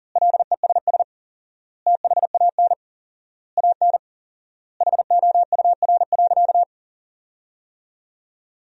Morse code, heard as a keyed tone, about 40 wpm, 700 Hz. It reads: LESS THAN AN HOUR.
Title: 4x Word-Spacing - Sentences from Top 500 Words - 40wpm